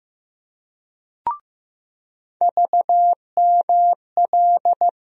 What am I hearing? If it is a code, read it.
Morse code, heard as a keyed tone, about 15 wpm, 700 Hz. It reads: VML
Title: USA Call Sign Postfixes - 3 Letters 15wpm